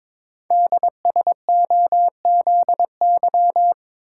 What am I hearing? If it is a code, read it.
Morse code, heard as a keyed tone, about 22 wpm, 700 Hz. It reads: DSOZY
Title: Code Groups - 5 Letters - 22wpm